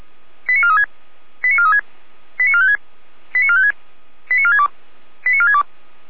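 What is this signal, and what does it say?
Five-tone selective call (ZVEI), heard as quick runs of five tones, tones of about 70 ms. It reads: 89436 89436 89456 89456 89462 89462
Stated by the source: Five-tone ZVEI selective call received on an SDR.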